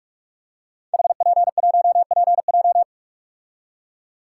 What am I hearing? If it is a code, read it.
Morse code, heard as a keyed tone, about 44 wpm, 700 Hz. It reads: HP1PJ
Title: Code Groups - 5 Letter:Numbers - 45wpm